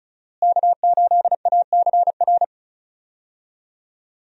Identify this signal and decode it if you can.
Morse code, heard as a keyed tone, about 36 wpm, 700 Hz. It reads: K8ACR
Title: USA Call Signs - 5 Characters - Repeated 2x - 35wpm